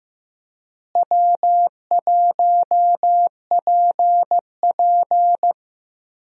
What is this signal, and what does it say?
Morse code, heard as a keyed tone, about 15 wpm, 700 Hz. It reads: W1PP
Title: USA Call Signs - 4 Characters - Repeated 2x - 15wpm